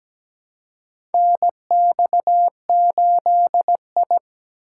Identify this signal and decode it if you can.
Morse code, heard as a keyed tone, about 17 wpm, 700 Hz. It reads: NX8I
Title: USA Call Signs - 4 Characters - 4x Character Spacing - 17wpm